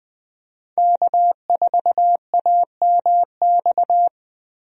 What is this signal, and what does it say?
Morse code, heard as a keyed tone, about 20 wpm, 700 Hz. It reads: K4AMX
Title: USA Call Signs - 5 Characters - 20wpm